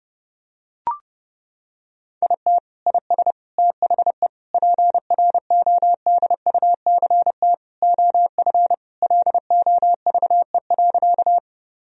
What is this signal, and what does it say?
Morse code, heard as a keyed tone, about 30 wpm, 700 Hz. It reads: IT IS THE PRODUCT OF LOVE.
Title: Sentences from Top 1000 Words - Review Part 2 - 30wpm